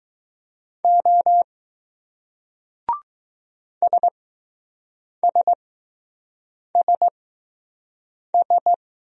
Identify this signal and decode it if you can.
Morse code, heard as a keyed tone, about 20 wpm, spaced as for 6 wpm, 700 Hz. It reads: O SSSS